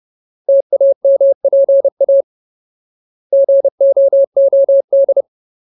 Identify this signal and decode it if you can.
Morse code, heard as a keyed tone, about 30 wpm, 550 Hz. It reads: TAMPA GOOD
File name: Ultimate Word List / 30WPM Ultimate